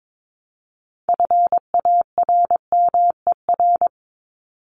Morse code, heard as a keyed tone, about 22 wpm, 700 Hz. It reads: FARMER
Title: Top 1751-2000 Words 22wpm